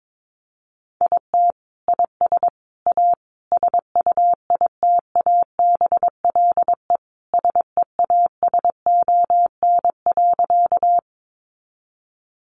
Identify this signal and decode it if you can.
Morse code, heard as a keyed tone, about 22 wpm, 700 Hz. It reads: IT IS A SUITABLE SEASON.
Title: Sentences from Top 1400 Words 22wpm